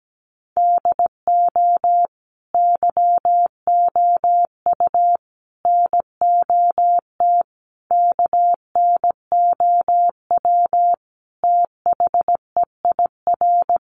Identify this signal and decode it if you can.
Morse code, heard as a keyed tone, about 17 wpm, 700 Hz. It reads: DO YOU NOT KNOW THEIR
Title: Sentences from Top 600 Words 17wpm